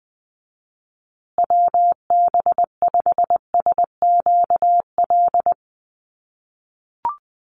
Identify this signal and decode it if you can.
Morse code, heard as a keyed tone, about 19 wpm, 700 Hz. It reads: WB5SQL
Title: USA Call Signs Repeated 3x 20wpm